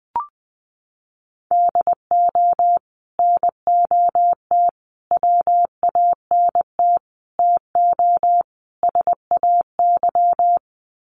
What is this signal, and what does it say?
Morse code, heard as a keyed tone, about 20 wpm, 700 Hz. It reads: DO NOT WANT TO SAY
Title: Sets of 5 Words Encore 20wpm